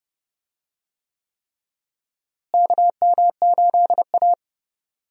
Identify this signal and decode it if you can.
Morse code, heard as a keyed tone, about 30 wpm, 700 Hz. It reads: KM8A
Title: USA Call Signs - 4 Characters - Repeated 3x - 30wpm